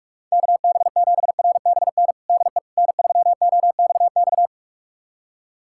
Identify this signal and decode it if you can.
Morse code, heard as a keyed tone, about 46 wpm, 700 Hz. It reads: KB7RBN DE N3OXX
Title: USA Callsign DE Callsign 45wpm